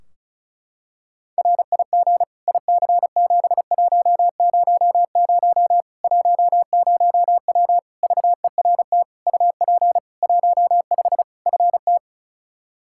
Morse code, heard as a keyed tone, about 36 wpm, 700 Hz. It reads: RIG IC7100 10W VERT UP 15 FT
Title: Practice QSO - Exchange 2 - Info - 35wpm